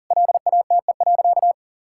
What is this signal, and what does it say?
Morse code, heard as a keyed tone, about 37 wpm, 700 Hz. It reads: LATE.
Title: Sentences from Top 1000 Words - Review Part 6 - 40wpm